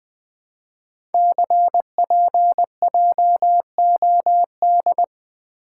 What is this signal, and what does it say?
Morse code, heard as a keyed tone, about 20 wpm, 700 Hz. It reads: CPJOD
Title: Code Groups - 5 Letters - 20wpm